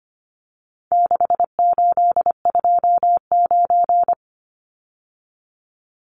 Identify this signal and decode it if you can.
Morse code, heard as a keyed tone, about 25 wpm, 700 Hz. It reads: 6829
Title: Numbers - 4-Digit 25wpm